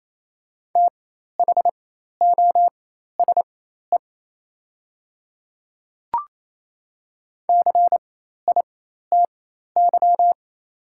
Morse code, heard as a keyed tone, about 28 wpm, spaced as for 13 wpm, 700 Hz. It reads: THOSE CITY
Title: 4x Character Spacing - Top 300 Words - 28wpm